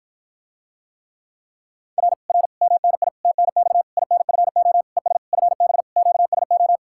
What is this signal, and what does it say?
Morse code, heard as a keyed tone, about 54 wpm, 700 Hz. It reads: R R MNI TNX INFO ES FB QSO